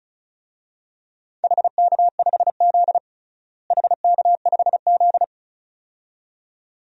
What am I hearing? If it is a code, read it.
Morse code, heard as a keyed tone, about 34 wpm, 700 Hz. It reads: HK5Z HK5Z
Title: DXCC-100 Call Signs - 4 Characters - Repeated 2x - 35wpm